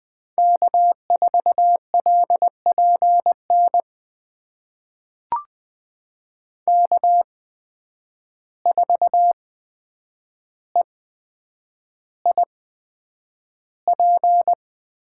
Morse code, heard as a keyed tone, about 20 wpm, spaced as for 9 wpm, 700 Hz. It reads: K4LPN K4EIP
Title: USA Call Signs - 5 Characters - 8x Character Spacing - 20wpm